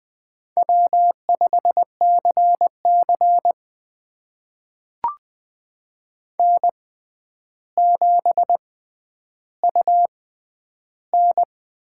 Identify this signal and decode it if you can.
Morse code, heard as a keyed tone, about 20 wpm, spaced as for 10 wpm, 700 Hz. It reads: W5CC N7UN
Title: USA Call Signs - 4 Characters - 6x Character Spacing - 20wpm